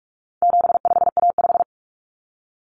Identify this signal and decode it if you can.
Morse code, heard as a keyed tone, about 44 wpm, 700 Hz. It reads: 75A5
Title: Common Cut Numbers - 4-Digit 45wpm